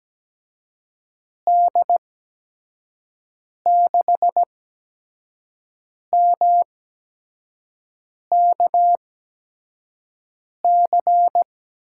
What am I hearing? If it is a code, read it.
Morse code, heard as a keyed tone, about 17 wpm, spaced as for 5 wpm, 700 Hz. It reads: D6MKC
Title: USA Call Signs - 6 Characters - 8x Character Spacing - 17wpm